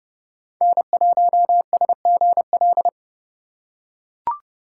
Morse code, heard as a keyed tone, about 30 wpm, 700 Hz. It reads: N1SGL
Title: USA Call Signs - 5 Characters - 2x Character Spacing - 30wpm